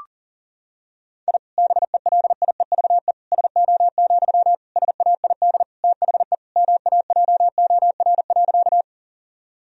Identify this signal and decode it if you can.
Morse code, heard as a keyed tone, about 37 wpm, 700 Hz. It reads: I BELIEVE SO, SAID THE MAJOR.
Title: Sentences from Top 1000 Words - Review Part 4 - 40wpm